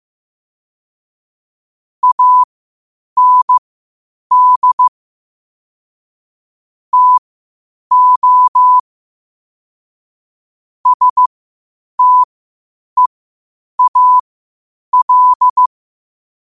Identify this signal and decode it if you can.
Morse code, heard as a keyed tone, about 15 wpm, spaced as for 8 wpm, 1000 Hz. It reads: AND TO STEAL
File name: qotdfiles / 8wpm